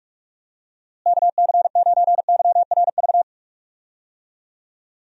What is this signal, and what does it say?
Morse code, heard as a keyed tone, about 41 wpm, 700 Hz. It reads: KC9YRV
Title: USA Call Signs - 6 Characters - Repeated 2x - 45wpm